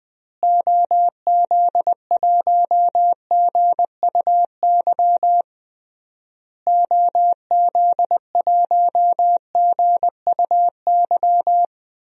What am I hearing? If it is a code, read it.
Morse code, heard as a keyed tone, about 20 wpm, 700 Hz. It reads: OZ1GUY OZ1GUY
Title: DXCC-100 Call Signs - 6 Characters - Repeated 3x - 20wpm